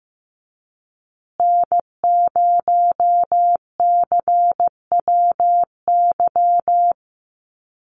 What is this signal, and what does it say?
Morse code, heard as a keyed tone, about 15 wpm, 700 Hz. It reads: N0CWY
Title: USA Call Signs Repeated 3x 15wpm